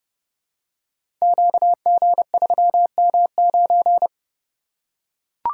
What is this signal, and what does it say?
Morse code, heard as a keyed tone, about 28 wpm, 700 Hz. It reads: QG3M9
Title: Code Groups - 5 Letter:Numbers - 30wpm